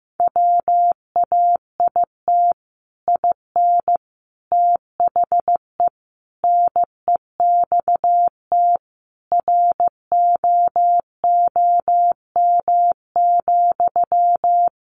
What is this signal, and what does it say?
Morse code, heard as a keyed tone, about 15 wpm, 700 Hz. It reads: WAIT IN THE NEXT ROOM,
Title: Sentences from Top 1500 Words 15wpm